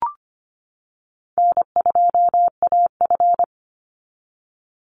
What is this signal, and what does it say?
Morse code, heard as a keyed tone, about 25 wpm, 700 Hz. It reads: N2AF
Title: NA SOTA Chaser Callsigns 25wpm